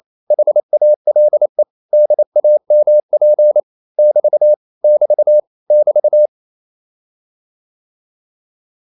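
Morse code, heard as a keyed tone, about 28 wpm, 600 Hz. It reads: HALE DAMP <BT> <BT> <BT>
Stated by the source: Danish short words sent in Morse at 28 wpm, from CW med Gnister.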